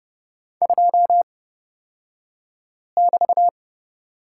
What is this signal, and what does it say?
Morse code, heard as a keyed tone, about 30 wpm, spaced as for 5 wpm, 700 Hz. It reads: 2<BT>
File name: Lesson 36d - BT - Two Characters 30wpm